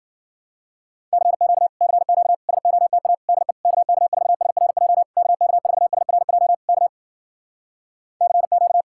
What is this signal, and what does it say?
Morse code, heard as a keyed tone, about 56 wpm, 700 Hz. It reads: CQ CQ SOTA DE KK4SRJ KK4SRJ K CQ